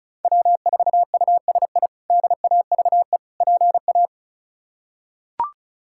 Morse code, heard as a keyed tone, about 34 wpm, 700 Hz. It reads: W4USI DAVE PA
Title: K1USN SST QSO Practice - 35wpm